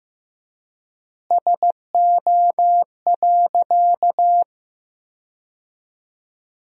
Morse code, heard as a keyed tone, about 15 wpm, 700 Hz. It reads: SO.
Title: 3x Word-Spacing - Sentences from Top 300 Words - 15wpm